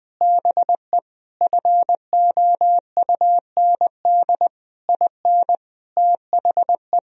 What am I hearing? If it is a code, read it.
Morse code, heard as a keyed tone, about 20 wpm, 700 Hz. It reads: BE FOUND IN THE
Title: Sentences from Top 1000 Words - Review Part 1 - 20wpm